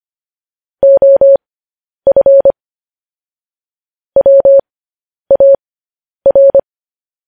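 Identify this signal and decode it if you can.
Morse code, heard as a keyed tone, about 25 wpm, spaced as for 10 wpm, 550 Hz. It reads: OF WAR
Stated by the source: Morse code news headlines sent at 10 wpm.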